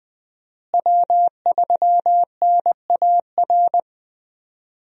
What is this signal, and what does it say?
Morse code, heard as a keyed tone, about 20 wpm, 700 Hz. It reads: W3NAR
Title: USA Call Signs - 5 Characters - Repeated 3x - 20wpm